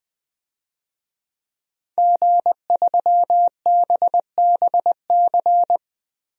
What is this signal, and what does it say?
Morse code, heard as a keyed tone, about 21 wpm, 700 Hz. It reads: G3BBC
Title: DXCC-100 Call Signs - 5 Characters - Repeated 3x - 20wpm